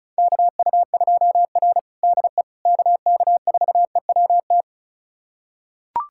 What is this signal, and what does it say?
Morse code, heard as a keyed tone, about 36 wpm, 700 Hz. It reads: KU2R DE KK4EWT
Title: USA Callsign DE Callsign 35wpm